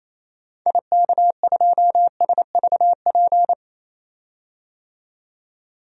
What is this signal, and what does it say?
Morse code, heard as a keyed tone, about 28 wpm, 700 Hz. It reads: IK2SVP
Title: DXCC-100 Call Signs - 6 Characters - 28wpm